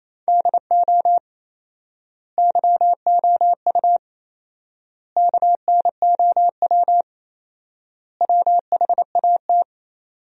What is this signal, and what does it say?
Morse code, heard as a keyed tone, about 28 wpm, 700 Hz. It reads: DO YOU KNOW WHAT
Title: 4x Word-Spacing - Sentences from Top 100 Words - 28wpm